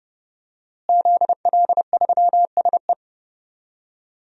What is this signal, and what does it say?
Morse code, heard as a keyed tone, about 30 wpm, 700 Hz. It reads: ZL3SE
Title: DXCC-100 Call Signs - 5 Characters - Repeated 2x - 30wpm